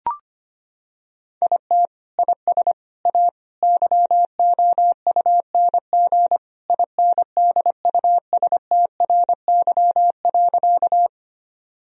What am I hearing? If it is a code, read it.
Morse code, heard as a keyed tone, about 25 wpm, 700 Hz. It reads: IT IS A YOUNG INDUSTRY.
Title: Sentences from Top 1000 Words - Review Part 4 - 25wpm